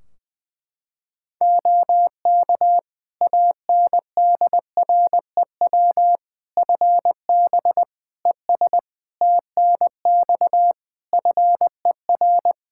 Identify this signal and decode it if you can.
Morse code, heard as a keyed tone, about 20 wpm, 700 Hz. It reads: OK ANDREW FB ES TNX FER
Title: Practice QSO - Exchange 2 - Info - 20wpm